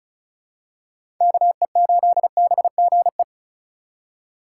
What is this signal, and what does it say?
Morse code, heard as a keyed tone, about 36 wpm, 700 Hz. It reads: KE8BGE
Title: USA Call Signs - 6 Characters - 3x Character Spacing - 35wpm